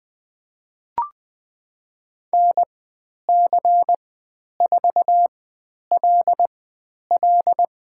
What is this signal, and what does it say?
Morse code, heard as a keyed tone, about 20 wpm, spaced as for 10 wpm, 700 Hz. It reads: NC4LL
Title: 095 Lesson 16g - Letter c - Call Signs 20wpm